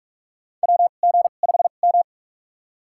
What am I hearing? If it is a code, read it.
Morse code, heard as a keyed tone, about 43 wpm, spaced as for 32 wpm, 700 Hz. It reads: WG5M
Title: USA Call Signs - 2x Character Spacing - 45wpm